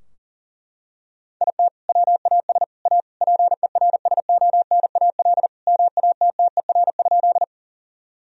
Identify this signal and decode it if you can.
Morse code, heard as a keyed tone, about 40 wpm, 700 Hz. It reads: IT WAS A PERSONAL MATTER?
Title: Sentences from Top 1000 Words - Review Part 4 - 40wpm